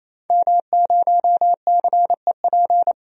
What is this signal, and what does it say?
Morse code, heard as a keyed tone, about 28 wpm, 700 Hz. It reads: M0CEP
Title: DXCC-100 Call Signs - 5 Characters - 28wpm